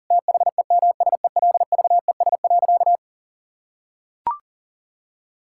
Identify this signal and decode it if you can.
Morse code, heard as a keyed tone, about 39 wpm, 700 Hz. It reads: THEMSELVES.
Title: Sentences from Top 1000 Words - Review Part 5 - 40wpm